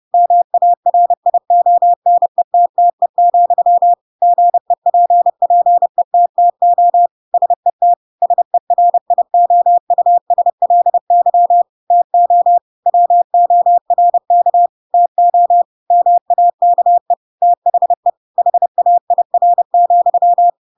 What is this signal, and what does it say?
Morse code, heard as a keyed tone, about 30 wpm, 700 Hz. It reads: MARIONETTE, GEPPETTO SET SERIOUSLY TO WORK TO MAKE THE HAIR,
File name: Pinocchio / Pinocchio ch3 30 30 ik1hge